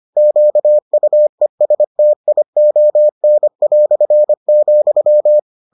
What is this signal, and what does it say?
Morse code, heard as a keyed tone, about 25 wpm, 600 Hz. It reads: QUESTION",